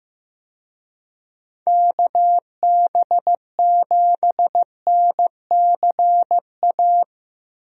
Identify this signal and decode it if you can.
Morse code, heard as a keyed tone, about 15 wpm, 700 Hz. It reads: KB7NCA
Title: USA Call Signs - 6 Characters - Repeated 2x - 15wpm